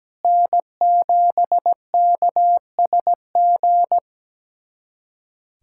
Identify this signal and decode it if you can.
Morse code, heard as a keyed tone, about 17 wpm, 700 Hz. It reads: N7KSG
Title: USA Callsign DE Callsign - 5 Characters - 17wpm